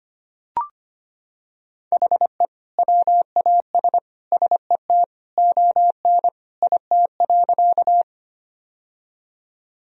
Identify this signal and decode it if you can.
Morse code, heard as a keyed tone, about 25 wpm, 700 Hz. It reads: HE WAS SET ON IT.
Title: Sentences from Top 1000 Words - Review Part 1 - 25wpm